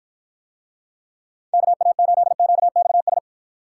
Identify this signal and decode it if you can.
Morse code, heard as a keyed tone, about 53 wpm, 700 Hz. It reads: KA8QXS